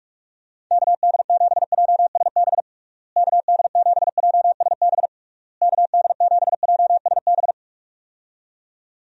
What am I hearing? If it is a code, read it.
Morse code, heard as a keyed tone, about 41 wpm, 700 Hz. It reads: KD7JSB KD7JSB KD7JSB
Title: USA Call Signs - 6 Characters - Repeated 3x - 45wpm